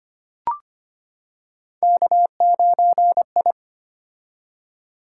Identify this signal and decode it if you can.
Morse code, heard as a keyed tone, about 25 wpm, 700 Hz. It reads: K9I
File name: Three Letter-Number Combinations - 25wpm